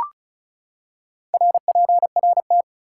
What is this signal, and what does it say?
Morse code, heard as a keyed tone, about 36 wpm, 700 Hz. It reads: RPRT
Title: Top 180 QSO Elements 35wpm